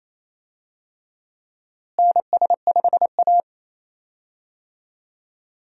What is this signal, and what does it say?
Morse code, heard as a keyed tone, about 28 wpm, 700 Hz. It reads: NS5A
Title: USA Call Signs - 4 Characters - Repeated 3x - 28wpm